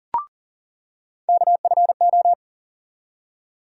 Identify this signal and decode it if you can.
Morse code, heard as a keyed tone, about 40 wpm, 700 Hz. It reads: KFO
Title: USA Call Sign Postfixes - 3 Letters 40wpm